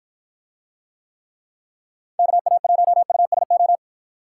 Morse code, heard as a keyed tone, about 51 wpm, 700 Hz. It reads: KA1USO